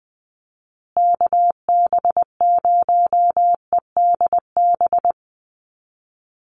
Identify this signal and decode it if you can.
Morse code, heard as a keyed tone, about 20 wpm, 700 Hz. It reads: KB0EDB
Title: USA Call Signs Repeated 3x 20wpm